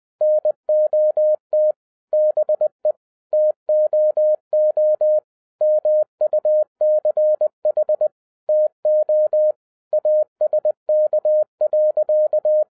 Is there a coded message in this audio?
Yes: Morse code from 0.2 to 12.6 s, heard as a keyed tone, about 20 wpm, 610 Hz. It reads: NOT BE TOO MUCH TO ASK.